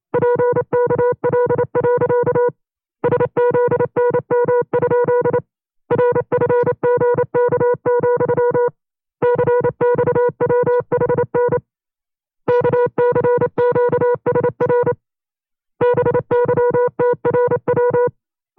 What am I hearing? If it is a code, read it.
Morse code, heard as a keyed tone, about 27 wpm, 490 Hz. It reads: PKL. SZNM? RFGK, CXWHN KCQSR BYTRW
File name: FreeFiles / CW-practice / 25WPM-side1